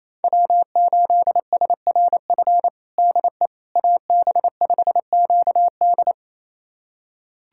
Morse code, heard as a keyed tone, about 27 wpm, 700 Hz. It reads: W8SRF DE AB5QD
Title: USA Callsign DE Callsign - 5 Characters - 28wpm